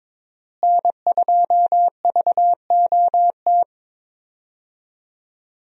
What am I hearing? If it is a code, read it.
Morse code, heard as a keyed tone, about 21 wpm, 700 Hz. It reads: N2VOT
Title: USA Call Signs - 5 Characters - Repeated 3x - 22wpm